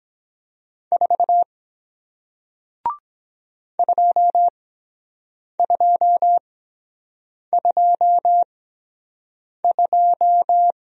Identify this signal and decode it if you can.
Morse code, heard as a keyed tone, about 23 wpm, spaced as for 7 wpm, 700 Hz. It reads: 4 2222